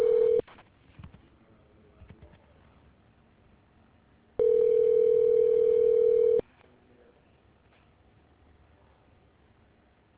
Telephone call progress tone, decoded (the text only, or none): ringback tone